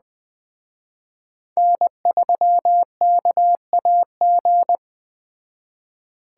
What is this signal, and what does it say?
Morse code, heard as a keyed tone, about 20 wpm, 700 Hz. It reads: N3KAG
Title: Speed Racing - USA 1x3 Call Signs 17wpm